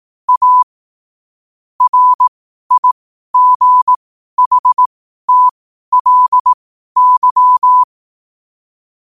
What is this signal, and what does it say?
Morse code, heard as a keyed tone, about 18 wpm, spaced as for 12 wpm, 1000 Hz. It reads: A RIGHTLY
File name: qotdfiles / 12wpm